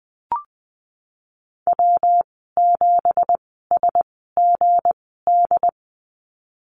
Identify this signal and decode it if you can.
Morse code, heard as a keyed tone, about 20 wpm, spaced as for 15 wpm, 700 Hz. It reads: W7SGD
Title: 186 Lesson 30g - Letter b - Call Signs 20wpm